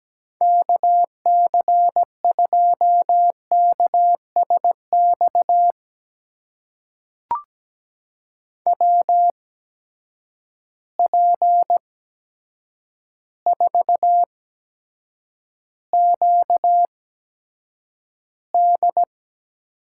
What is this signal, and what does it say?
Morse code, heard as a keyed tone, about 17 wpm, spaced as for 8 wpm, 700 Hz. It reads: KC2KSX WP4QD
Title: USA Call Signs - 8x Character Spacing - 17wpm